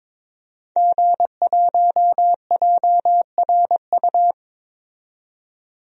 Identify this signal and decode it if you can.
Morse code, heard as a keyed tone, about 22 wpm, 700 Hz. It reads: G1JRU
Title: DXCC-100 Call Signs - 5 Characters - 22wpm